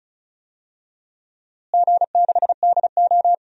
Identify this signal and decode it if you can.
Morse code, heard as a keyed tone, about 34 wpm, 700 Hz. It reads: G6DO
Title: DXCC-100 Call Signs - 4 Characters - Repeated 2x - 35wpm